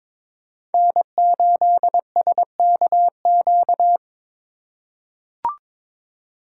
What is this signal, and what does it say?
Morse code, heard as a keyed tone, about 23 wpm, 700 Hz. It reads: N8SKQ
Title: USA Call Signs - 3x Character Spacing - 22wpm